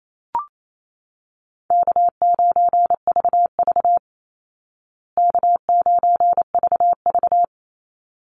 Morse code, heard as a keyed tone, about 28 wpm, 700 Hz. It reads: K9VV K9VV
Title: Call Signs Repeated 3x - 28wpm